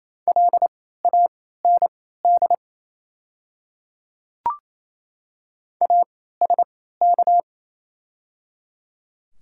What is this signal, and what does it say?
Morse code, heard as a keyed tone, about 28 wpm, spaced as for 16 wpm, 690 Hz. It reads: LAND ASK
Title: 3x Character Spacing - Top 200 Words - 28wpm